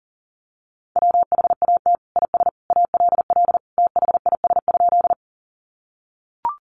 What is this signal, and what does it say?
Morse code, heard as a keyed tone, about 40 wpm, 700 Hz. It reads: WHAT IS ALL THIS?
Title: Sentences from Top 100 Words 40wpm